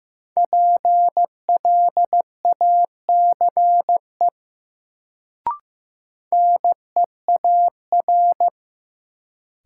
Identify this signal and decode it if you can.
Morse code, heard as a keyed tone, about 15 wpm, 700 Hz. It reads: PLACE NEAR
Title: Sentences from Top 200 Words - New Word Drill - 15wpm